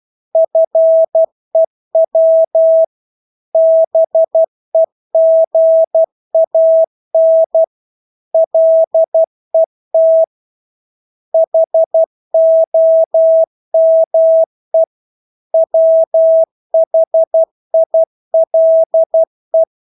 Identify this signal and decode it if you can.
Morse code, heard as a keyed tone, about 12 wpm, 650 Hz. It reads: FEW BEGAN LET HOME WHILE
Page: Engelske ord 12 wpm